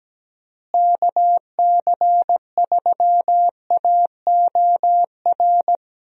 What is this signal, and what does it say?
Morse code, heard as a keyed tone, about 17 wpm, 700 Hz. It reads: KC3AOR
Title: USA Call Signs - 6 Characters - Repeated 2x - 17wpm